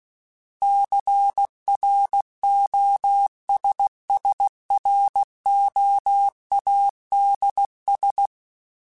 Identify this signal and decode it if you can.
Morse code, heard as a keyed tone, about 16 wpm, 800 Hz. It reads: CROSSROADS
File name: fromSMs / FromSMs mp3s / Cities@15